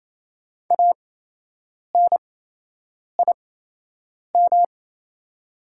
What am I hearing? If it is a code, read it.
Morse code, heard as a keyed tone, about 28 wpm, spaced as for 8 wpm, 700 Hz. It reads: ANIM
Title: 8x Character Spacing - Top 300 Words - 28wpm